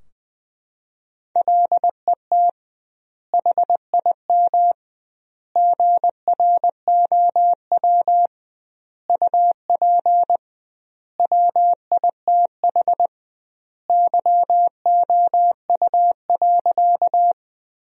Morse code, heard as a keyed tone, about 20 wpm, 700 Hz. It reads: LET HIM GROW UP WITH YOU.
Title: 2x Word-Spacing - Sentences from Top 300 Words - 20wpm